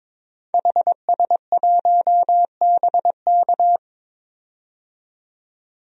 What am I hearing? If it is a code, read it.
Morse code, heard as a keyed tone, about 21 wpm, 700 Hz. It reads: HS1BK
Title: DXCC-100 Call Signs - 5 Characters - 22wpm